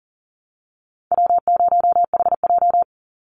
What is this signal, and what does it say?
Morse code, heard as a keyed tone, about 40 wpm, 700 Hz. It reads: W0HJ
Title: USA Call Signs 40wpm